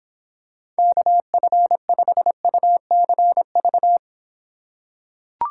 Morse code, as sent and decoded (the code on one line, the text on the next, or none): -.- ..-. ..... ..- -.-. ...-
KF5UCV